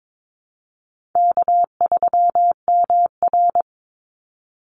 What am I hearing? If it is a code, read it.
Morse code, heard as a keyed tone, about 22 wpm, 700 Hz. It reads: K3MR
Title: Easy Call Signs Repeated 3x - 22wpm